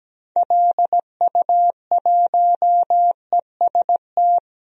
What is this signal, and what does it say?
Morse code, heard as a keyed tone, about 17 wpm, 700 Hz. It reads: LU1EST
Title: DXCC-100 Call Signs - 6 Characters - Repeated 3x - 17wpm